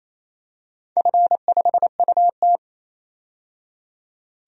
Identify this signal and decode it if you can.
Morse code, heard as a keyed tone, about 28 wpm, 700 Hz. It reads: F5UT
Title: DXCC-100 Call Signs - 4 Characters - 28wpm